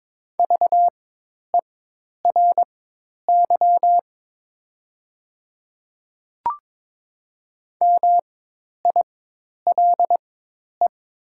Morse code, heard as a keyed tone, about 23 wpm, spaced as for 10 wpm, 700 Hz. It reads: VERY MILE
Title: 4x Character Spacing - Top 500 Words - 22wpm